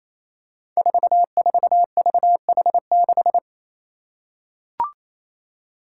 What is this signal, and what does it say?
Morse code, heard as a keyed tone, about 27 wpm, 700 Hz. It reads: 44VH6
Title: Five Challenging Letter-Number Combinations 28wpm